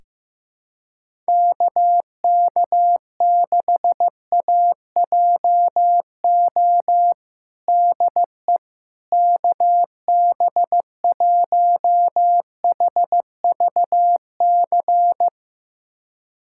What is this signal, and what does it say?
Morse code, heard as a keyed tone, about 15 wpm, 700 Hz. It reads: KK6AJO DE KB1HVC
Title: USA Callsign DE Callsign 15wpm